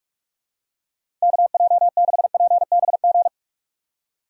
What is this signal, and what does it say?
Morse code, heard as a keyed tone, about 46 wpm, 700 Hz. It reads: KJ6PBG
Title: USA Call Signs - 2x Character Spacing - 45wpm